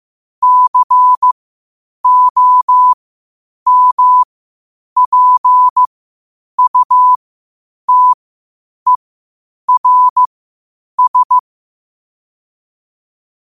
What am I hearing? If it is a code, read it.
Morse code, heard as a keyed tone, about 15 wpm, spaced as for 8 wpm, 1000 Hz. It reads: COMPUTERS
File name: qotdfiles / 8wpm